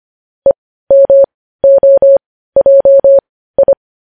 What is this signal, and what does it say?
Morse code, heard as a keyed tone, about 25 wpm, spaced as for 15 wpm, 550 Hz. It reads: EMOJI
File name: podcast / News headlines-1x-25x15